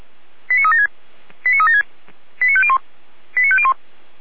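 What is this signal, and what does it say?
Five-tone selective call (ZVEI), heard as quick runs of five tones, tones of about 70 ms. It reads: 89376 89376 89501 89501
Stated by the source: ZVEI five-tone selective call 89042 received via SDR.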